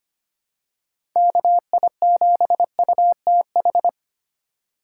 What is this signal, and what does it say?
Morse code, heard as a keyed tone, about 25 wpm, 700 Hz. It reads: KI7UTH
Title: USA Call Signs - 6 Characters - 6x Character Spacing - 25wpm